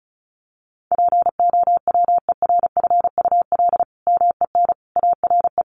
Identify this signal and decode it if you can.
Morse code, heard as a keyed tone, about 34 wpm, 700 Hz. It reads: POWERFUL MEN ARE
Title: Sentences from Top 1400 Words 35wpm